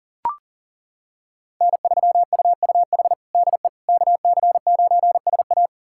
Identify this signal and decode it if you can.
Morse code, heard as a keyed tone, about 40 wpm, 700 Hz. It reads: N3UUH DE KC9SA
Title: USA Callsign DE Callsign - 5 Characters - 40wpm